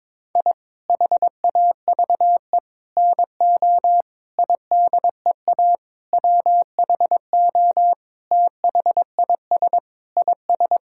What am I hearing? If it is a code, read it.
Morse code, heard as a keyed tone, about 21 wpm, 700 Hz. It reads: I HAVE NO IDEA WHO THIS IS
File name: Sentences with 9-Word Phrases - 22wpm